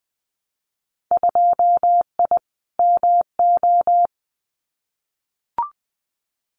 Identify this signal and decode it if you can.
Morse code, heard as a keyed tone, about 20 wpm, 700 Hz. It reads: 2I MO
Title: Winter Field Day Exchanges 20wpm